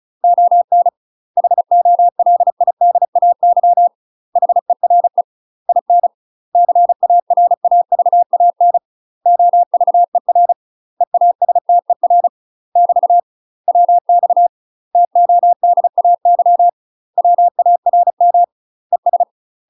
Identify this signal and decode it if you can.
Morse code, heard as a keyed tone, about 36 wpm, 700 Hz. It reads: ON HOLIDAY HERE IN CARAVAN OVER EASTER <BT> WX TODAY WARM ES
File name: UK MASTER QSOs TRAINING / 35WPM / UK MASTER QSOs 0038